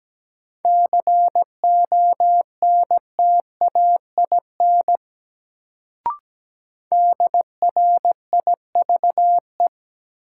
Morse code, heard as a keyed tone, about 17 wpm, 700 Hz. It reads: CONTAIN DRIVE
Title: Sentences from Top 500 Words - New Word Drill - 17wpm